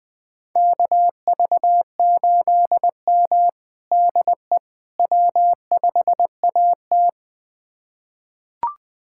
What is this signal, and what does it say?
Morse code, heard as a keyed tone, about 20 wpm, 700 Hz. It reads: KV8M DE W5AT
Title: USA Callsign DE Callsign - 4 Characters - 20wpm